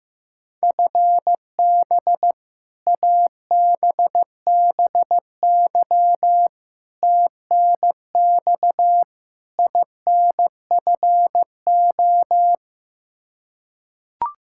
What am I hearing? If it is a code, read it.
Morse code, heard as a keyed tone, about 15 wpm, 700 Hz. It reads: FB ABBY TNX INFO
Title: Practice QSO - Exchange 3 - More Info - 15wpm